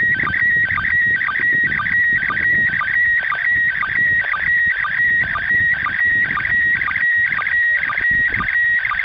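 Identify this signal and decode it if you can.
SSTV (slow-scan television), PD 120.
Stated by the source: SSTV transmission from the ISS.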